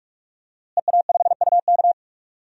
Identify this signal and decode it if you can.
Morse code, heard as a keyed tone, about 46 wpm, 700 Hz. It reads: EA5UK
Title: DXCC-100 Call Signs - 5 Characters - Repeated 3x - 45wpm